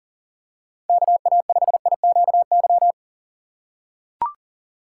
Morse code, heard as a keyed tone, about 40 wpm, 700 Hz. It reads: KA5IQY
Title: USA Call Signs - 6 Characters - 2x Character Spacing - 40wpm